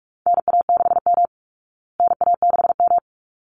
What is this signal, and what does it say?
Morse code, heard as a keyed tone, about 44 wpm, 700 Hz. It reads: NA6M NA6M
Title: Call Signs Repeated 3x - 45wpm